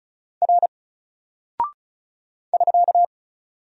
Morse code, heard as a keyed tone, about 35 wpm, 700 Hz. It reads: R <SK>
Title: Single Character (Letter:Number:Punctuation:Prosign) 35wpm